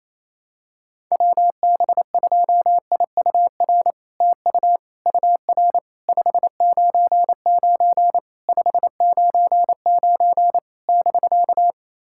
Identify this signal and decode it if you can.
Morse code, heard as a keyed tone, about 28 wpm, 700 Hz. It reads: WB2IUR TU UR 599 599 <BK>
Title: SOTA QSO Practice 28wpm